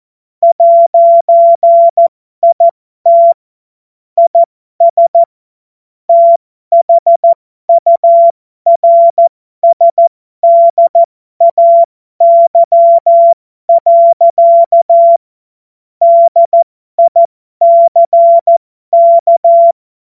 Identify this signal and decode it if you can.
Morse code, heard as a keyed tone, about 14 wpm, spaced as for 12 wpm, 680 Hz. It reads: 'IT IS THURSDAY. DICK
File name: Morsekurs / Witze